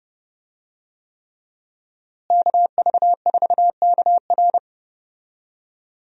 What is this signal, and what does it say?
Morse code, heard as a keyed tone, about 31 wpm, 700 Hz. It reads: KV4KR
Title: USA Call Signs - 5 Characters - Repeated 3x - 30wpm